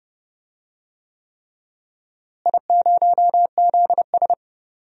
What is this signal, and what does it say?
Morse code, heard as a keyed tone, about 31 wpm, 700 Hz. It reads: I0ZS
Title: DXCC-100 Call Signs - 4 Characters - Repeated 3x - 30wpm